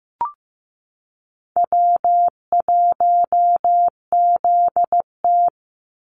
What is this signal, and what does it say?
Morse code, heard as a keyed tone, about 15 wpm, 700 Hz. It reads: W1ZT
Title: Call Signs Repeated 2x - 15wpm